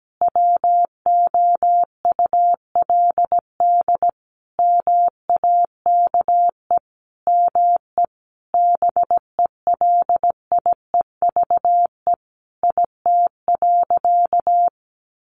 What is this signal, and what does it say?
Morse code, heard as a keyed tone, about 17 wpm, 700 Hz. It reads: WOULD MAKE ME BELIEVE IT.
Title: Sentences from Top 1300 Words 17wpm